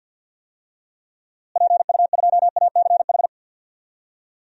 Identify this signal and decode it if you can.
Morse code, heard as a keyed tone, about 53 wpm, 700 Hz. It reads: PU2ACH
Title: DXCC-100 Call Signs - 6 Characters - Repeated 3x - 50wpm